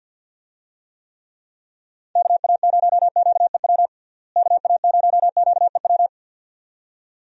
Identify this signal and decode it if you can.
Morse code, heard as a keyed tone, about 52 wpm, 700 Hz. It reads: KA0QEW KA0QEW
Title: USA Call Signs - 6 Characters - Repeated 2x - 50wpm